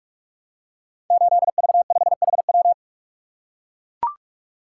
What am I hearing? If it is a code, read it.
Morse code, heard as a keyed tone, about 44 wpm, 690 Hz. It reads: 8V5HW
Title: Code Groups - 5 Letter:Numbers - 45wpm